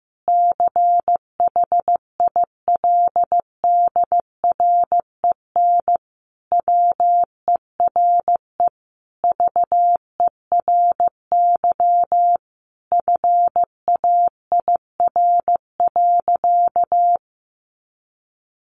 Morse code, heard as a keyed tone, about 15 wpm, 700 Hz. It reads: CHILDREN WERE VERY FAIR.
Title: Sentences from Top 1300 Words 15wpm